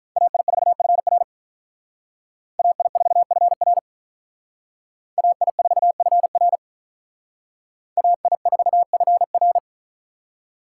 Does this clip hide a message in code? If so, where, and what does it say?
Yes: Morse code from 0.2 to 9.6 s, heard as a keyed tone, about 47 wpm, 700 Hz. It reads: AI4FR AI4FR AI4FR AI4FR